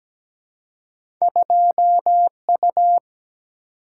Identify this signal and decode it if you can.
Morse code, heard as a keyed tone, about 17 wpm, 700 Hz. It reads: 2U
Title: Single-Letter Single-Number 17wpm